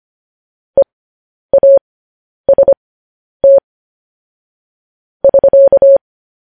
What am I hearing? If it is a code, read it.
Morse code, heard as a keyed tone, about 25 wpm, spaced as for 10 wpm, 550 Hz. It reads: EAST <SK>